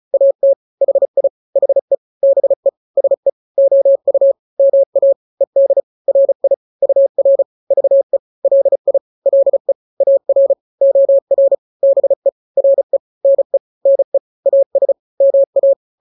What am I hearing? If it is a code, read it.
Morse code, heard as a keyed tone, about 35 wpm, 550 Hz. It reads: AT HI HE BE SE OU MA ED RI UR VE LI LE AR OR BE RE NE NE AS MA
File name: n-grams / 2-grams 35 30 WPM